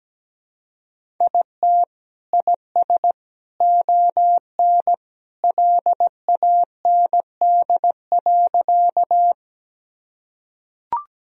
Morse code, heard as a keyed tone, about 17 wpm, 700 Hz. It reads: IT IS ON LAND.
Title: Sentences from Top 1000 Words - Review Part 1 - 17wpm